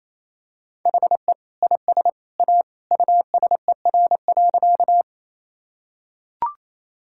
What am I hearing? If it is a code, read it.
Morse code, heard as a keyed tone, about 28 wpm, 700 Hz. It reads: HE IS A USER.
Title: Sentences from Top 1000 Words - Review Part 5 - 28wpm